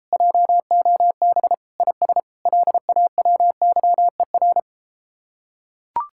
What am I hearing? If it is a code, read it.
Morse code, heard as a keyed tone, about 33 wpm, 700 Hz. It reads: JOB IS LAWYER